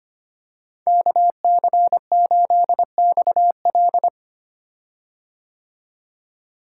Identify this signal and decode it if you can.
Morse code, heard as a keyed tone, about 25 wpm, 700 Hz. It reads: KC8XL
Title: Code Groups - 5 Letter:Numbers - 25wpm